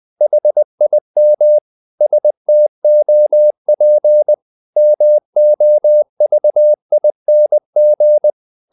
Morse code, heard as a keyed tone, about 20 wpm, 600 Hz. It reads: HIM STOP MOVING
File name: the old man and the sea / the old man and the sea part 1 20wpm